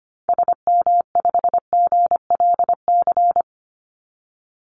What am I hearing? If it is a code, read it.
Morse code, heard as a keyed tone, about 25 wpm, 700 Hz. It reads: SM5GLC
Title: Call Signs - 25wpm